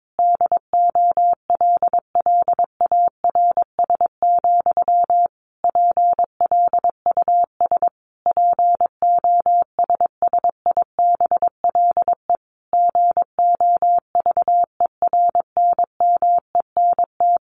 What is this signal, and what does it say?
Morse code, heard as a keyed tone, about 22 wpm, 700 Hz. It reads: DOLLARS, PLUS POSSIBLE GOVERNMENT
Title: A Sound Of Thunder 22wpm